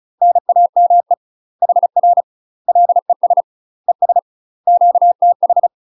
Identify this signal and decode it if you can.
Morse code, heard as a keyed tone, about 34 wpm, 700 Hz. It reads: NAME HR LES ES QTH
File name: UK MASTER QSOs TRAINING / 35WPM / UK MASTER QSOs 0024